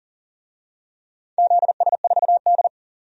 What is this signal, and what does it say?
Morse code, heard as a keyed tone, about 40 wpm, 700 Hz. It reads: ZS4D
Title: DXCC-100 Call Signs - 4 Characters - 40wpm